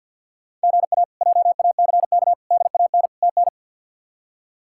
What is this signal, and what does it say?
Morse code, heard as a keyed tone, about 52 wpm, 700 Hz. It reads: GA JACK DAN TN
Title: K1USN SST QSO Practice - 50wpm